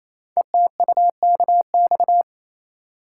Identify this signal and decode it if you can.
Morse code, heard as a keyed tone, about 29 wpm, 700 Hz. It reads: ETUKX
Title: Code Groups - 5 Letters - 28wpm